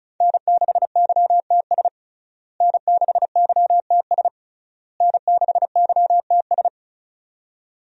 Morse code, heard as a keyed tone, about 36 wpm, 700 Hz. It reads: N6YTS N6YTS N6YTS
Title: USA Call Signs - 5 Characters - Repeated 3x - 35wpm